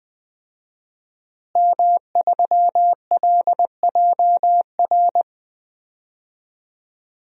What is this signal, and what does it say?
Morse code, heard as a keyed tone, about 20 wpm, 700 Hz. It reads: M3LJR
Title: DXCC-100 Call Signs - 5 Characters - Repeated 3x - 20wpm